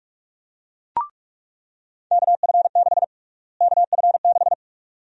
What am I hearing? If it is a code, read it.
Morse code, heard as a keyed tone, about 46 wpm, 700 Hz. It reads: KF6 KF6
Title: USA Call Sign Prefixes - Repeated 2x - 45wpm